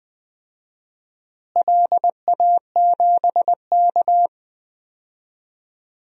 Morse code, heard as a keyed tone, about 20 wpm, 700 Hz. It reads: LA7K
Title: DXCC-100 Call Signs - 4 Characters - Repeated 3x - 20wpm